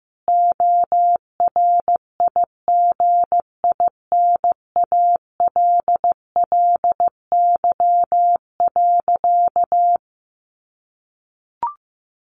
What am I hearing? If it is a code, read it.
Morse code, heard as a keyed tone, about 15 wpm, 700 Hz. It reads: ORIGINALLY.
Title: Sentences from Top 1500 Words 15wpm